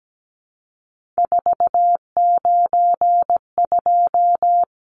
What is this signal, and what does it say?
Morse code, heard as a keyed tone, about 17 wpm, 700 Hz. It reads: 492
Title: Numbers - 3-Digit 17wpm